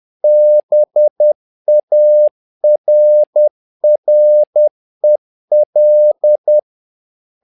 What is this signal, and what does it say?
Morse code, heard as a keyed tone, about 10 wpm, 600 Hz. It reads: BARREL
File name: learning center / tower trouble / tower trouble 10wpm